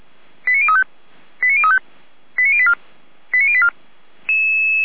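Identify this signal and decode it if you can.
Five-tone selective call (ZVEI), heard as quick runs of five tones, tones of about 70 ms. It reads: 89035 89035 89084 89084
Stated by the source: ZVEI five-tone selective call received by SDR.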